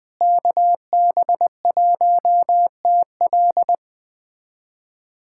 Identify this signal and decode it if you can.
Morse code, heard as a keyed tone, about 20 wpm, 700 Hz. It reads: KB1TL
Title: USA Callsign DE Callsign - 5 Characters - 20wpm